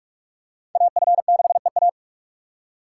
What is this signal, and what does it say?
Morse code, heard as a keyed tone, about 46 wpm, 700 Hz. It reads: AF6EA
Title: USA Call Signs - 2x Character Spacing - 45wpm